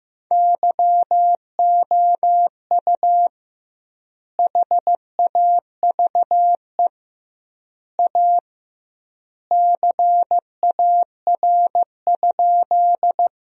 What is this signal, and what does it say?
Morse code, heard as a keyed tone, about 15 wpm, 700 Hz. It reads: YOU HAVE A CAR?
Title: 2x Word-Spacing - Sentences from Top 300 Words - 15wpm